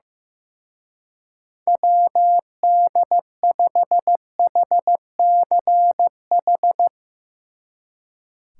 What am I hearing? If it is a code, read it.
Morse code, heard as a keyed tone, about 16 wpm, 700 Hz. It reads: WD5HCH
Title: Speed Racing - USA 2x3 Call Signs 15wpm